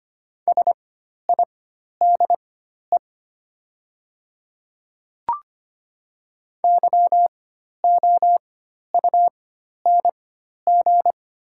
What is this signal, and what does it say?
Morse code, heard as a keyed tone, about 25 wpm, spaced as for 12 wpm, 700 Hz. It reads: SIDE YOUNG
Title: 4x Character Spacing - Top 500 Words - 25wpm